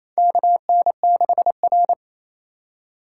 Morse code, heard as a keyed tone, about 28 wpm, 700 Hz. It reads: KN6R
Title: USA Call Signs - 4 Characters - 6x Character Spacing - 28wpm